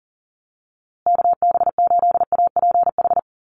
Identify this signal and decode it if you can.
Morse code, heard as a keyed tone, about 40 wpm, 700 Hz. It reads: KB8APH
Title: USA Call Signs Repeated 3x 40wpm